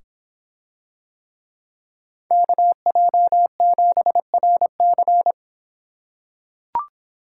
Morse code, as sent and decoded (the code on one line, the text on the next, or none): -.- .--- --... .-. -.-.
KJ7RC